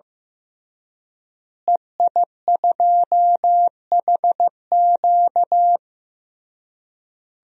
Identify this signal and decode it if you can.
Morse code, heard as a keyed tone, about 15 wpm, 700 Hz. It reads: EI2HQ
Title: DXCC-100 Call Signs - 5 Characters - Repeated 3x - 15wpm